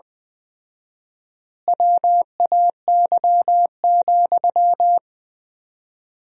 Morse code, heard as a keyed tone, about 20 wpm, 700 Hz. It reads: WAY,
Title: 4x Word-Spacing - Sentences from Top 400 Words - 20wpm